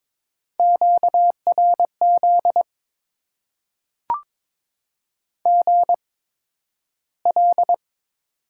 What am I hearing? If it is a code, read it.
Morse code, heard as a keyed tone, about 22 wpm, spaced as for 12 wpm, 700 Hz. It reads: QRZ GL
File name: Top 90 QSO Elements - 8x Character Spacing - 22wpm